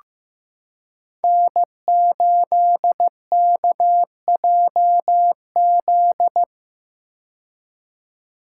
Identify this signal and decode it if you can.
Morse code, heard as a keyed tone, about 15 wpm, 700 Hz. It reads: N8KJZ
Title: Code Groups - 5 Letter:Numbers - 15wpm